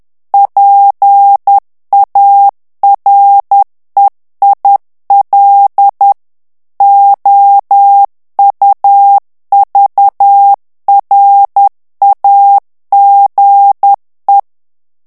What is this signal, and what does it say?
Morse code, heard as a keyed tone, about 10 wpm, 800 Hz. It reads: PAREIL OUVRAGE
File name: CW / FICHIERS-MP3 / FRANCAIS TEXTE CLAIR 2 VITESSE 10 MOTS MINUTES